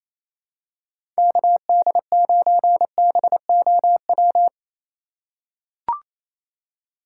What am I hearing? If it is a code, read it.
Morse code, heard as a keyed tone, about 28 wpm, 700 Hz. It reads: KD9BOW
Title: USA Call Signs - 6 Characters - 2x Character Spacing - 28wpm